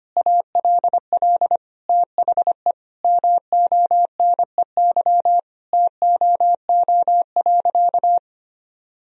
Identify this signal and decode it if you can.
Morse code, heard as a keyed tone, about 25 wpm, 700 Hz. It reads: ALL THE MONEY TOO.
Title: Sentences from Top 1000 Words - Review Part 4 - 25wpm